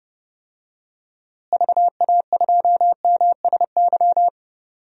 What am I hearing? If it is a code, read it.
Morse code, heard as a keyed tone, about 32 wpm, 700 Hz. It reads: VA2MSY
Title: DXCC-100 Call Signs - 6 Characters - Repeated 3x - 30wpm